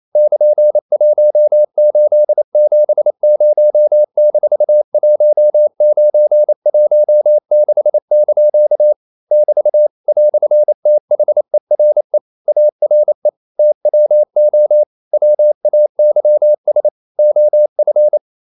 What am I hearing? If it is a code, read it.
Morse code, heard as a keyed tone, about 28 wpm, 600 Hz. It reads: <KN>1870-1916) <BT> "THERE ARE TWO WAYS OF
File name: CW / Practice / EnglishQuotations0228